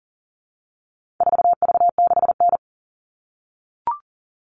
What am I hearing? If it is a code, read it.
Morse code, heard as a keyed tone, about 40 wpm, 700 Hz. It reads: 4V6N
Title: All Cut Numbers - 4-Digit 40wpm